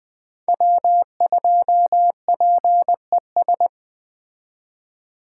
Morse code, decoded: W2PES

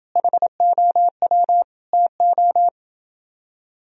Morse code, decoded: HOW TO

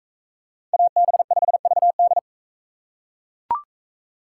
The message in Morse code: AB5VD

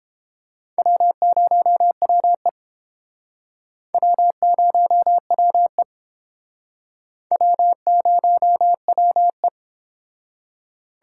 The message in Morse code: W0WE W0WE W0WE

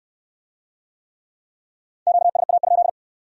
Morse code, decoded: QSA?

